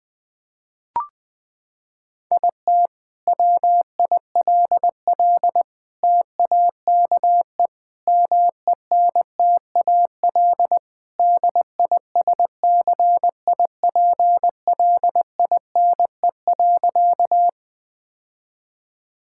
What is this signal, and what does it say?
Morse code, heard as a keyed tone, about 21 wpm, 700 Hz. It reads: IT WILL TAKE MENTAL DISCIPLINE.
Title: Sentences from Top 1000 Words - Review Part 6 - 20wpm